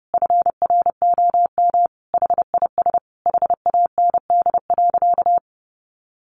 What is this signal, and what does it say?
Morse code, heard as a keyed tone, about 30 wpm, 700 Hz. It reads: FROM HIS HAND.
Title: Sentences from Top 1200 Words 30wpm